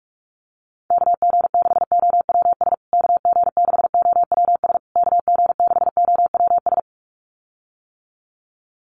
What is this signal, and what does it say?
Morse code, heard as a keyed tone, about 44 wpm, 700 Hz. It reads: KG6OWS KG6OWS KG6OWS
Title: USA Call Signs Repeated 3x 45wpm